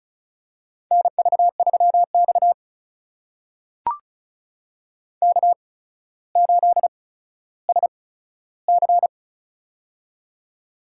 Morse code, heard as a keyed tone, about 36 wpm, spaced as for 15 wpm, 700 Hz. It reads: NV3X K8SC
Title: USA Call Signs - 4 Characters - 8x Character Spacing - 35wpm